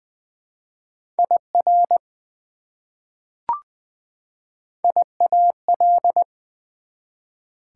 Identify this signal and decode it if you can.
Morse code, heard as a keyed tone, about 20 wpm, 700 Hz. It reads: IR IAL
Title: Prefixes and Suffixes 20wpm